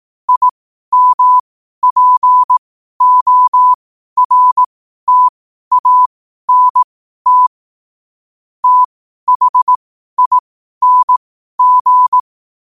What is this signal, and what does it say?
Morse code, heard as a keyed tone, about 18 wpm, spaced as for 12 wpm, 1000 Hz. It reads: IMPORTANT THING
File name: qotdfiles / 12wpm